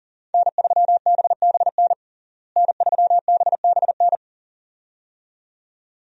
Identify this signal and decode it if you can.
Morse code, heard as a keyed tone, about 40 wpm, 700 Hz. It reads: N3BBN N3BBN
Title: USA Call Signs - 5 Characters - Repeated 2x - 40wpm